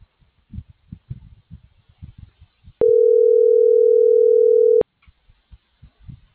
Ringback tone (North America), 440 + 480 Hz.